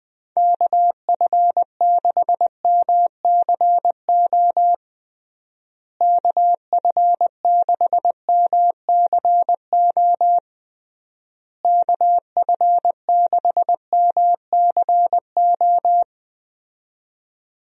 Morse code, heard as a keyed tone, about 20 wpm, 700 Hz. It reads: KF6MCO KF6MCO KF6MCO
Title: USA Call Signs - 6 Characters - Repeated 3x - 20wpm